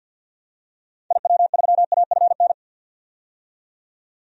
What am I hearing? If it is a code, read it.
Morse code, heard as a keyed tone, about 46 wpm, 700 Hz. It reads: IW3AFN
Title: DXCC-100 Call Signs - 6 Characters - 50wpm